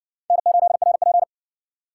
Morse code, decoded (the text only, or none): N8RP